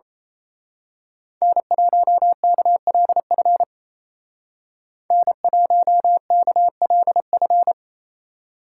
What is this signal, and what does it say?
Morse code, heard as a keyed tone, about 30 wpm, 700 Hz. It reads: N1KLF N1KLF
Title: Speed Racing - USA 1x3 Call Signs 28wpm